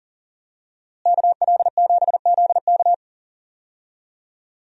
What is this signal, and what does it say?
Morse code, heard as a keyed tone, about 40 wpm, 700 Hz. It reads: KL7ZK
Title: USA Call Signs - 5 Characters - Repeated 3x - 40wpm